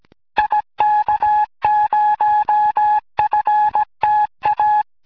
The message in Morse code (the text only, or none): IK0FTA